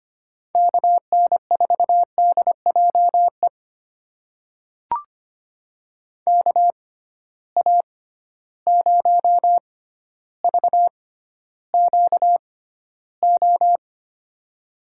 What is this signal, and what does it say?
Morse code, heard as a keyed tone, about 25 wpm, spaced as for 13 wpm, 700 Hz. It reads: KN4DJE KA0VQO
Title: USA Call Signs - 6 Characters - 6x Character Spacing - 25wpm